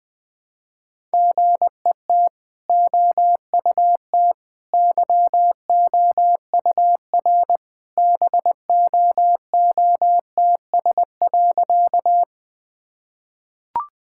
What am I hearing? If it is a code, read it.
Morse code, heard as a keyed tone, about 20 wpm, 700 Hz. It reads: GET OUT YOUR BOOTS.
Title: Easy Sentences with Hints - 20wpm